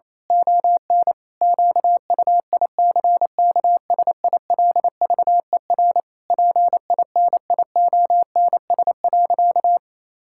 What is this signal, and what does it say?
Morse code, heard as a keyed tone, about 27 wpm, 700 Hz. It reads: ON QUICKSILVER PINIONS.
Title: Pangram Challenge v2 - 28wpm